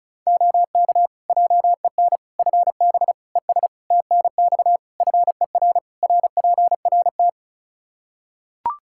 Morse code, heard as a keyed tone, about 36 wpm, 700 Hz. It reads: OK JEN FB ES TNX FER RPRT
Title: Practice QSO - Exchange 2 - Info - 35wpm